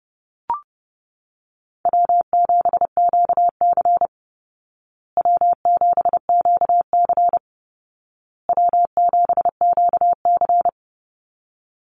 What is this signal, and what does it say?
Morse code, heard as a keyed tone, about 30 wpm, 700 Hz. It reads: W7QC W7QC W7QC
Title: Call Signs Repeated 2x - 30wpm